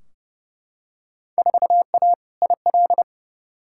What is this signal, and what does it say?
Morse code, heard as a keyed tone, about 30 wpm, 700 Hz. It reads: 4A IL
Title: Field Day Exchanges - 30wpm